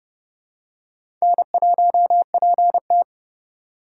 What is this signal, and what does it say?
Morse code, heard as a keyed tone, about 31 wpm, 700 Hz. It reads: N1PT
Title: USA Call Signs - 4 Characters - 2x Character Spacing - 30wpm